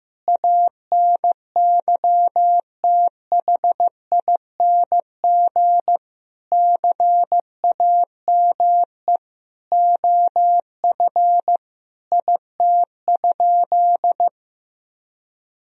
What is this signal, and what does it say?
Morse code, heard as a keyed tone, about 15 wpm, 700 Hz. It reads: ANYTHING CAME OF IT?
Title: Sentences from Top 1000 Words - Review Part 3 - 15wpm